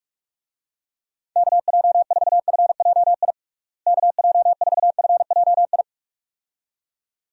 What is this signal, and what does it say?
Morse code, heard as a keyed tone, about 47 wpm, 700 Hz. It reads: KJ4FJI KJ4FJI
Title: USA Call Signs - 6 Characters - Repeated 2x - 45wpm